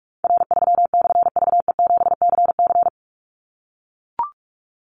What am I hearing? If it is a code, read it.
Morse code, heard as a keyed tone, about 44 wpm, 700 Hz. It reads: R3/VE7CC
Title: Call Signs - 45wpm